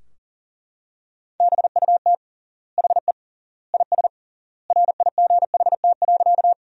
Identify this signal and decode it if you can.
Morse code, heard as a keyed tone, about 37 wpm, 700 Hz. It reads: BUT HE IS RIGHT.
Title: 3x Word-Spacing - Sentences from Top 200 Words - 40wpm